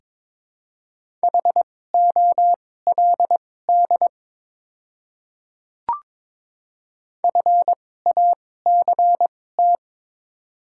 Morse code, heard as a keyed tone, about 22 wpm, spaced as for 16 wpm, 700 Hz. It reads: HOLD FACT
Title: 2x Character Spacing - Top 500 Words - 22wpm